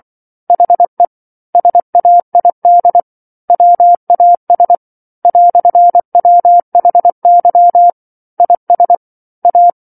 Morse code, heard as a keyed tone, about 24 wpm, 700 Hz. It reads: HE SAID WAS "WHY IS A
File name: 07-Chapter-07 24 WPM